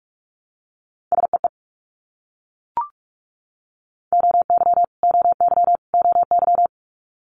Morse code, heard as a keyed tone, about 42 wpm, 700 Hz. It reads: SEE OY OY OY
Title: Multi-letter Phonemes Repeated 3x 45wpm